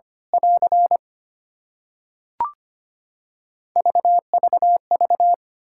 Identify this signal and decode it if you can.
Morse code, heard as a keyed tone, about 25 wpm, 700 Hz. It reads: <AR> VVV
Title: Practice QSO - Exchange 2 - Info - 25wpm